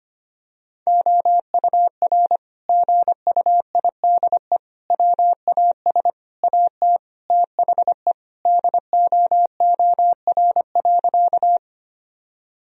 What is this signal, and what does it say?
Morse code, heard as a keyed tone, about 25 wpm, 700 Hz. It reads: OUR GUIDE WAS AT THE DOOR.
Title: Sentences from Top 1000 Words - Review Part 3 - 25wpm